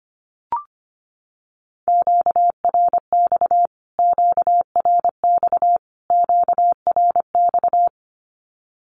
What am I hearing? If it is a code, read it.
Morse code, heard as a keyed tone, about 25 wpm, 700 Hz. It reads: QRX QRX QRX
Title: Q-Codes Repeated 3x 25wpm